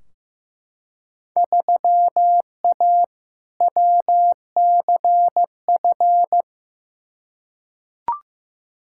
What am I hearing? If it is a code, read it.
Morse code, heard as a keyed tone, about 15 wpm, 700 Hz. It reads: 3A WCF
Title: Field Day Exchanges Encore - 15wpm